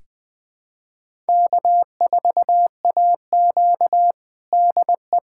Morse code, heard as a keyed tone, about 21 wpm, 700 Hz. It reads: K4AQ DE
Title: USA Callsign DE Callsign - 4 Characters - 20wpm